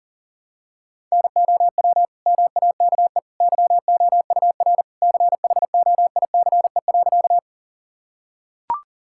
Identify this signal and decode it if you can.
Morse code, heard as a keyed tone, about 40 wpm, 700 Hz. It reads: NOW MAKE YOUR CHOICE.
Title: Sentences from Top 1000 Words - Review Part 6 - 40wpm